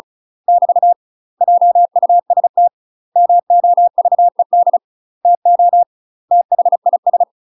Morse code, heard as a keyed tone, about 36 wpm, 700 Hz. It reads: <BT> JUST MOVED TO THIS
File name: UK MASTER QSOs TRAINING / 35WPM / UK MASTER QSOs 0007